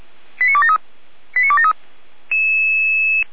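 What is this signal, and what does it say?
Five-tone selective call (ZVEI), heard as quick runs of five tones, tones of about 70 ms. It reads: 89273 89273
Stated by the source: ZVEI five-tone selective call 89001 received by SDR at feuerwehr-orlamuende (fire brigade Orlamünde).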